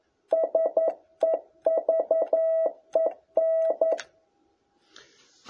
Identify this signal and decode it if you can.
Morse code, heard as a keyed tone, about 11 wpm, 660 Hz. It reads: SEVEN
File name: jh8cbh / 2021cw